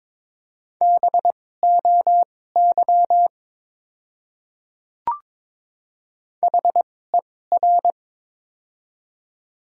Morse code, heard as a keyed tone, about 22 wpm, spaced as for 16 wpm, 690 Hz. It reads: BOY HER
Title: 2x Character Spacing - Top 200 Words - 22wpm